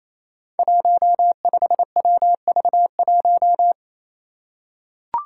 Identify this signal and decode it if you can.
Morse code, heard as a keyed tone, about 27 wpm, 700 Hz. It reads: 15WV1
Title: Five Challenging Letter-Number Combinations 28wpm